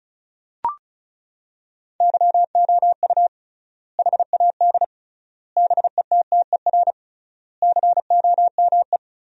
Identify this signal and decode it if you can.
Morse code, heard as a keyed tone, about 36 wpm, 700 Hz. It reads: YOU HAD BETTER COME
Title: 3x Word-Spacing - Sentences from Top 400 Words - 35wpm